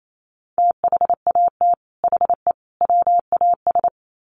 Morse code, heard as a keyed tone, about 28 wpm, 700 Hz. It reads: THAT HE WAS
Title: Sets of 3 Words 28wpm